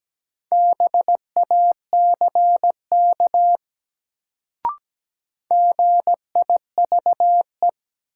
Morse code, heard as a keyed tone, about 17 wpm, 700 Hz. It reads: BACK GIVE
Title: Sentences from Top 200 Words - New Word Drill - 17wpm